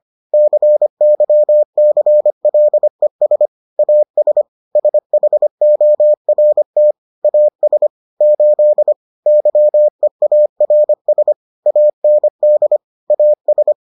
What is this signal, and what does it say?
Morse code, heard as a keyed tone, about 25 wpm, 600 Hz. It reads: CYCLES AS SHORT AS 8 YEARS AND AS